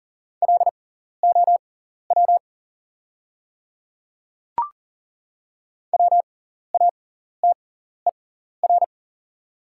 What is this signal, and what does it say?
Morse code, heard as a keyed tone, about 40 wpm, spaced as for 14 wpm, 690 Hz. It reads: LOW WATER
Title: 6x Character Spacing - Top 200 Words - 40wpm